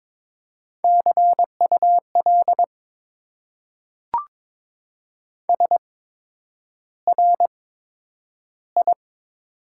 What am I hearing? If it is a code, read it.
Morse code, heard as a keyed tone, about 22 wpm, spaced as for 9 wpm, 700 Hz. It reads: CUL SRI
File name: Top 90 QSO Elements - 8x Character Spacing - 22wpm